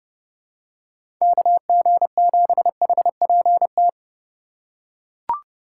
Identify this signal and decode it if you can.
Morse code, heard as a keyed tone, about 31 wpm, 700 Hz. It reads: KG7HPT
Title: USA Call Signs - 6 Characters - 4x Character Spacing - 30wpm